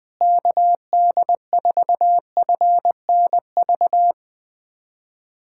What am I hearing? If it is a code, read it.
Morse code, heard as a keyed tone, about 20 wpm, 700 Hz. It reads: KD4FNV
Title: USA Callsign DE Callsign 20wpm